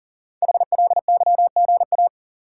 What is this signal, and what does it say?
Morse code, heard as a keyed tone, about 38 wpm, 700 Hz. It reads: HLYGA
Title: Code Groups - 5 Letters - 40wpm